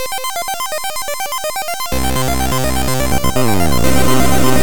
There is no code.